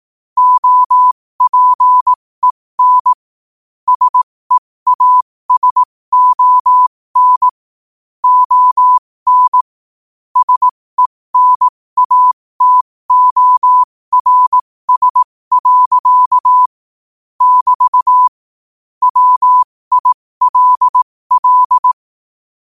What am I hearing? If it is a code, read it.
Morse code, heard as a keyed tone, about 18 wpm, spaced as for 15 wpm, 1000 Hz. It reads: OPEN SEASON ON SENATORS. <BT> WILL
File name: qotdfiles / 15wpm